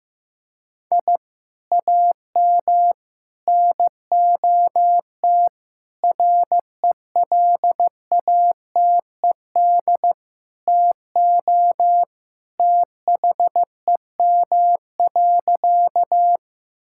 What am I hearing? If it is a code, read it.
Morse code, heard as a keyed tone, about 14 wpm, 700 Hz. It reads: I AM NOT RELATED TO THEM.
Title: Sentences from Top 1000 Words - Review Part 5 - 15wpm